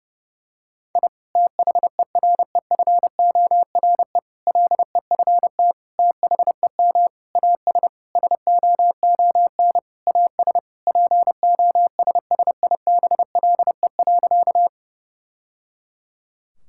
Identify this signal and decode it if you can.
Morse code, heard as a keyed tone, about 30 wpm, 700 Hz. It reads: I THEREFORE LEFT THEM AS SOON AS POSSIBLE.
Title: Sentences from Top 1000 Words - Review Part 5 - 30wpm